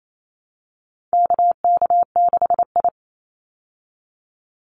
Morse code, heard as a keyed tone, about 28 wpm, 700 Hz. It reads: KK6I
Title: Call Signs Repeated 2x - 28wpm